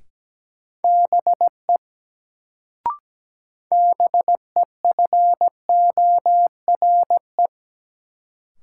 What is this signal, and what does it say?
Morse code, heard as a keyed tone, about 17 wpm, 700 Hz. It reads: BE BEFORE
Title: Sets of 2 Words - New Word Drill - 17wpm